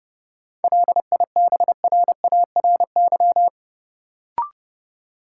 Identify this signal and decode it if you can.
Morse code, heard as a keyed tone, about 30 wpm, 700 Hz. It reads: LIBRARY
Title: Sentences from Top 600 Words - New Word Drill - 30wpm